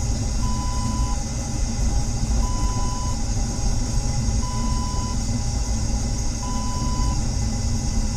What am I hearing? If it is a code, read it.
There is no code.